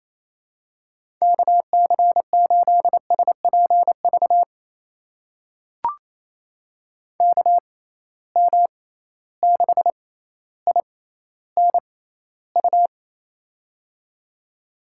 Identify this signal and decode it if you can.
Morse code, heard as a keyed tone, about 28 wpm, spaced as for 14 wpm, 700 Hz. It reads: KC8SPV KM6INU
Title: USA Call Signs - 6 Characters - 6x Character Spacing - 28wpm